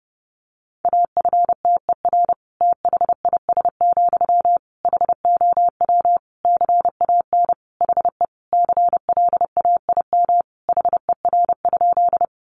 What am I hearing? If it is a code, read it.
Morse code, heard as a keyed tone, about 28 wpm, 700 Hz. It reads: AFTER THIS, HOW CAN HE CLAIM HER?
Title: Sentences from Top 1500 Words 30wpm